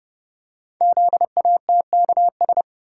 Morse code, heard as a keyed tone, about 29 wpm, 700 Hz. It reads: ZATKS
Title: Code Groups - 5 Letters - 30wpm